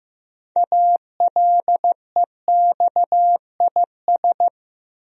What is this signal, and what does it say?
Morse code, heard as a keyed tone, about 15 wpm, 700 Hz. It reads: ALEXIS
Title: USA Common Names - 15wpm